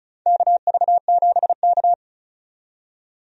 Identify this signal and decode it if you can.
Morse code, heard as a keyed tone, about 36 wpm, 700 Hz. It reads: KV7K
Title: USA Call Signs - 4 Characters - 35wpm